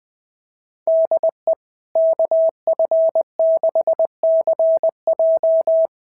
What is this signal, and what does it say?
Morse code, heard as a keyed tone, about 20 wpm, 650 Hz. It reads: DE KF6CJ
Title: SOTA QSO Practice 20wpm